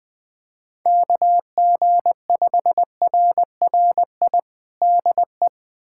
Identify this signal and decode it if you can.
Morse code, heard as a keyed tone, about 21 wpm, 700 Hz. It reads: KG5RRI DE
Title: USA Callsign DE Callsign 20wpm